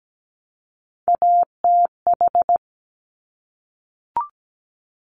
Morse code, heard as a keyed tone, about 17 wpm, 700 Hz. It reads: ATH
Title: Top Three Letter Combinations in Words 17wpm